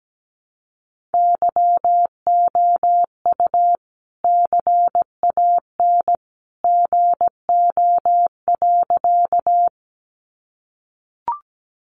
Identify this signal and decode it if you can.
Morse code, heard as a keyed tone, about 17 wpm, 700 Hz. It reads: YOU CAN GO.
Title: Sentences from Top 100 Words 17wpm